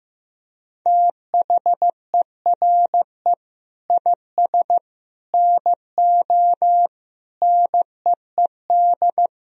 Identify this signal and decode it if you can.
Morse code, heard as a keyed tone, about 15 wpm, 700 Hz. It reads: THERE IS NO NEED
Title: Sentences from Top 1000 Words - Review Part 3 - 15wpm